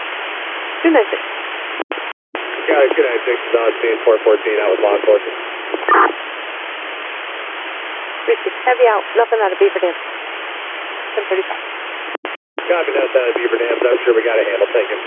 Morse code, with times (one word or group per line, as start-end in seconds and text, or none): none